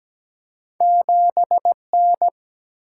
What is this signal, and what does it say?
Morse code, heard as a keyed tone, about 17 wpm, 700 Hz. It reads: 7N
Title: Single-Letter Single-Number 17wpm